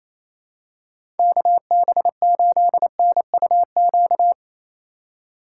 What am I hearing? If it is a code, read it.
Morse code, heard as a keyed tone, about 28 wpm, 700 Hz. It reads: KB8NUQ
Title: USA Call Signs - 6 Characters - 3x Character Spacing - 28wpm